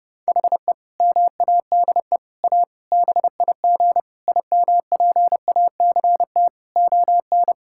Morse code, heard as a keyed tone, about 32 wpm, 700 Hz. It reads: HE MADE A BIG IMPACT ON
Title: Sentences from Top 1000 Words - Review Part 6 - 30wpm